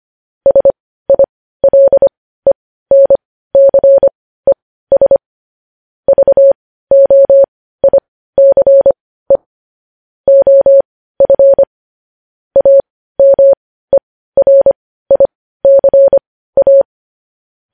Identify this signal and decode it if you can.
Morse code, heard as a keyed tone, about 25 wpm, spaced as for 15 wpm, 550 Hz. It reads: SILENCES VOICE OF AMERICA
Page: news-headlines-in-morse-code-at-15-wpm